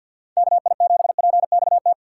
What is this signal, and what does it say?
Morse code, heard as a keyed tone, about 47 wpm, 700 Hz. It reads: KI7PXT
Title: USA Call Signs - 6 Characters - 8x Character Spacing - 50wpm